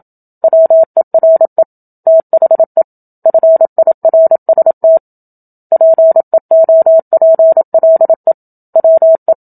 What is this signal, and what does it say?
Morse code, heard as a keyed tone, about 27 wpm, 650 Hz. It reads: WERE THE FIRST PEOPLE WE
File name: Book Two / 01-Chapter 1 - 27 WPM